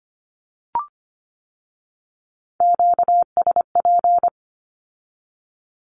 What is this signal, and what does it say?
Morse code, heard as a keyed tone, about 25 wpm, 700 Hz. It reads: QSP
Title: Q-Codes with Spoken Meaning - 25wpm